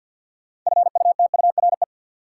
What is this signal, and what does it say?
Morse code, heard as a keyed tone, about 48 wpm, 700 Hz. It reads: FUTURE
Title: Sets of 3 Words - New Word Drill - 50wpm